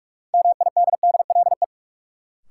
Morse code, heard as a keyed tone, about 46 wpm, 700 Hz. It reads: MIDDLE
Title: Sets of 2 Words - New Word Drill - 45wpm